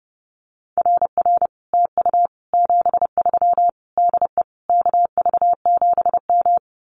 Morse code, heard as a keyed tone, about 30 wpm, 700 Hz. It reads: RR TU 73 DE KV7M
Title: Common QSO Phrases Encore 30wpm